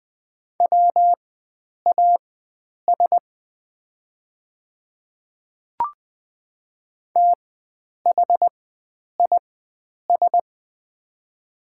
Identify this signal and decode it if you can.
Morse code, heard as a keyed tone, about 20 wpm, spaced as for 9 wpm, 700 Hz. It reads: WAS THIS
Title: 4x Character Spacing - Top 100 Words - 20wpm